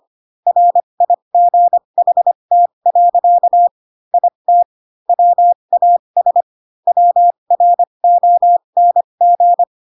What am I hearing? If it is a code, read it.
Morse code, heard as a keyed tone, about 25 wpm, spaced as for 22 wpm, 700 Hz. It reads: RIGHT. IT WAS WRONG